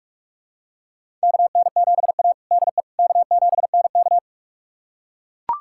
Morse code, heard as a keyed tone, about 41 wpm, 700 Hz. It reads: KN7A DE K7NK
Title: USA Callsign DE Callsign - 4 Characters - 45wpm